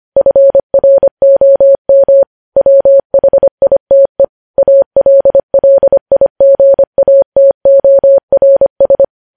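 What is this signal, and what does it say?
Morse code, heard as a keyed tone, about 25 wpm, 550 Hz. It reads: FROM WHITE ALLIGATORS
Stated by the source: Morse code practice of news headlines sent at 25 wpm.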